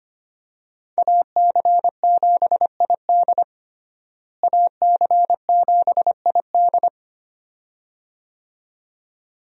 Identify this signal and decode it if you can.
Morse code, heard as a keyed tone, about 25 wpm, 700 Hz. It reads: AC7ID AC7ID
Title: USA Call Signs - 5 Characters - Repeated 2x - 25wpm